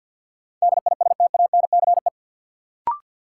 Morse code, heard as a keyed tone, about 53 wpm, 700 Hz. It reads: DISTANCE
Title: Sentences from Top 1200 Words - New Word Drill Alt - 50wpm